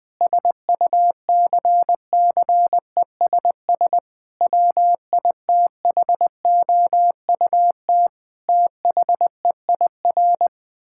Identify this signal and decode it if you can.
Morse code, heard as a keyed tone, about 20 wpm, 700 Hz. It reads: SUCCESS WITHOUT THEIR
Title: Sentences from Top 1000 Words - Review Part 6 - 20wpm